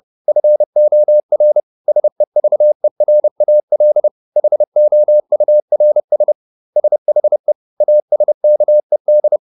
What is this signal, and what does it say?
Morse code, heard as a keyed tone, about 30 wpm, 600 Hz. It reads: FOR SEVERAL HOURS SHE ASKED